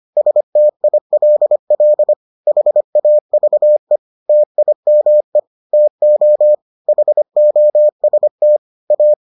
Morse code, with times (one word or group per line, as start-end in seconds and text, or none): 0.2-2.1 STILL
2.5-4.0 HAVE
4.3-5.4 TIME
5.7-6.5 TO
6.9-8.6 HOST
8.9-9.1 A